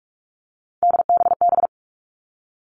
Morse code, heard as a keyed tone, about 46 wpm, 700 Hz. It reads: DBB
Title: Three Challenging Letter-Number Combinations 45wpm